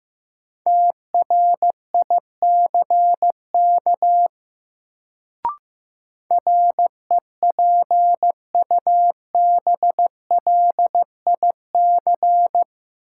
Morse code, heard as a keyed tone, about 15 wpm, 700 Hz. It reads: TRICK REPUBLIC
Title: Sentences from Top 1500 Words - New Word Drill - 15wpm